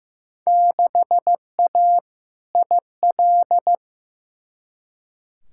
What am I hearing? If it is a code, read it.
Morse code, heard as a keyed tone, about 15 wpm, 700 Hz. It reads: 6A IL
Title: Field Day Exchanges - 15wpm